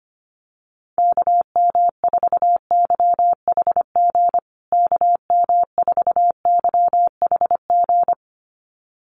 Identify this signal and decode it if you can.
Morse code, heard as a keyed tone, about 25 wpm, 700 Hz. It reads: KM4YHG KM4YHG
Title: USA Call Signs Repeated 2x 25wpm